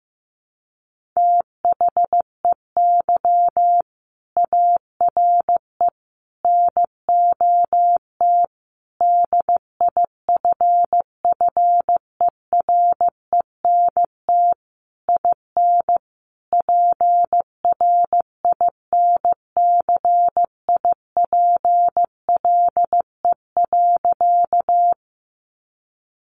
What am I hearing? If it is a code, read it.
Morse code, heard as a keyed tone, about 15 wpm, 700 Hz. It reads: THEY ARE NOT DIFFERENT IN PRINCIPLE.
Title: Sentences from Top 1400 Words 15wpm